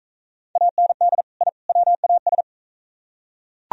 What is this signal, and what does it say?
Morse code, heard as a keyed tone, about 41 wpm, 700 Hz. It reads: AND I WAS